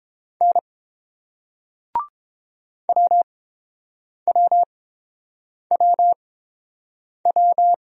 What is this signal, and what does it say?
Morse code, heard as a keyed tone, about 29 wpm, spaced as for 7 wpm, 700 Hz. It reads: N WWWW